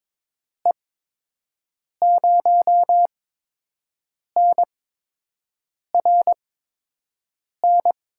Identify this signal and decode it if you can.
Morse code, heard as a keyed tone, about 23 wpm, spaced as for 6 wpm, 700 Hz. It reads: E0NRN
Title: USA Call Signs - 6 Characters - 8x Character Spacing - 22wpm